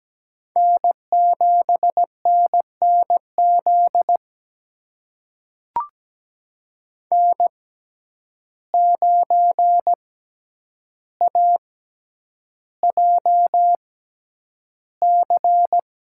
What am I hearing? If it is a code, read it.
Morse code, heard as a keyed tone, about 17 wpm, spaced as for 9 wpm, 700 Hz. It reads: N7NNZ N9AJC
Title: USA Call Signs - 5 Characters - 6x Character Spacing - 17wpm